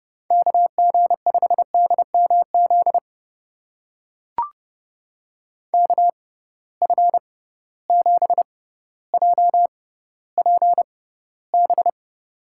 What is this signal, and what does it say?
Morse code, heard as a keyed tone, about 32 wpm, spaced as for 16 wpm, 700 Hz. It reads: KG5DMZ KF7JPB
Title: USA Call Signs - 6 Characters - 6x Character Spacing - 30wpm